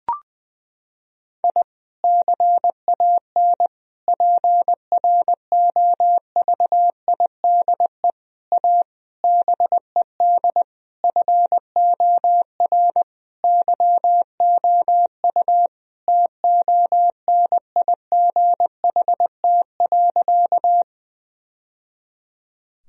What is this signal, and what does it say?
Morse code, heard as a keyed tone, about 21 wpm, 700 Hz. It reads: I CAN PROVIDE A BED FOR YOU TONIGHT.
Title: Sentences from Top 1000 Words - Review Part 3 - 20wpm